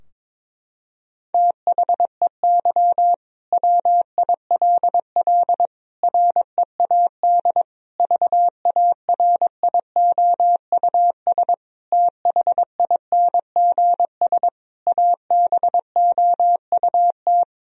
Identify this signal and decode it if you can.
Morse code, heard as a keyed tone, about 22 wpm, 700 Hz. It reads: THEY WILL READ VARIOUS THINGS ABOUT
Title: Sentences from Top 1000 Words - Review Part 4 - 22wpm